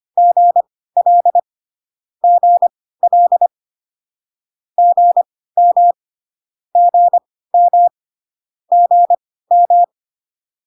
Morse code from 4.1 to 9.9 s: GM GM GM